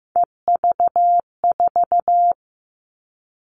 Morse code, heard as a keyed tone, about 15 wpm, 700 Hz. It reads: EV4
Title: All Cut Numbers - 3-Digit 15wpm